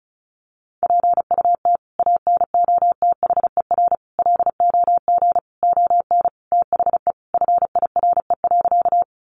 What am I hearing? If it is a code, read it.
Morse code, heard as a keyed tone, about 34 wpm, 700 Hz. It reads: PUT ANOTHER LOG ON THE FIRE.
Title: Sentences from Top 1400 Words 35wpm